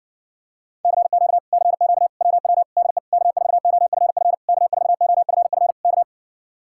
Morse code, heard as a keyed tone, about 60 wpm, 700 Hz. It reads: CQ CQ WW DE K4OFF K4OFF K